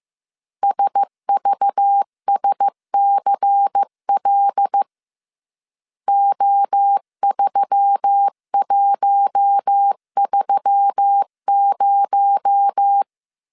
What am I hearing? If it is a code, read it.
Morse code, heard as a keyed tone, about 15 wpm, 800 Hz. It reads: SVSCL O3130
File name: CW / DET / 1 6 1500C S B 14wpm 2958